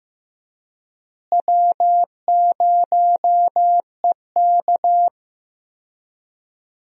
Morse code, heard as a keyed tone, about 15 wpm, 700 Hz. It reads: W0EK
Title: USA Call Signs - 4 Characters - Repeated 3x - 15wpm